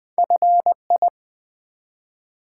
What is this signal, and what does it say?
Morse code, heard as a keyed tone, about 20 wpm, 700 Hz. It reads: FI
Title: Build A Word - 1 Letter At A Time - 5-Letter Words - 20wpm